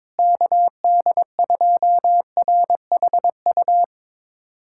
Morse code, heard as a keyed tone, about 21 wpm, 700 Hz. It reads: KD2RHU
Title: USA Callsign DE Callsign - 6 Characters - 22wpm